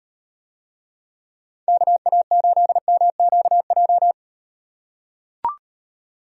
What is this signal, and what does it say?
Morse code, heard as a keyed tone, about 38 wpm, 700 Hz. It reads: KA8MQJ